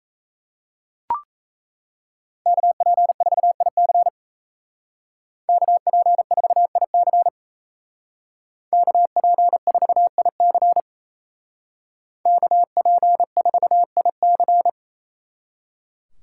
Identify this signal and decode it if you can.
Morse code, heard as a keyed tone, about 36 wpm, 700 Hz. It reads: KP4IC KP4IC KP4IC KP4IC